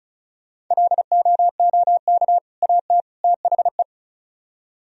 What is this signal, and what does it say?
Morse code, heard as a keyed tone, about 46 wpm, spaced as for 41 wpm, 700 Hz. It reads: LOOK AT THE
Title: Speed-Racing - Sets of 3 Words 35wpm